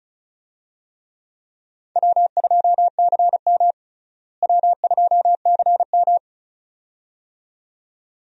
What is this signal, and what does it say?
Morse code, heard as a keyed tone, about 34 wpm, 700 Hz. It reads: W2CM W2CM
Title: USA Call Signs - 4 Characters - Repeated 2x - 35wpm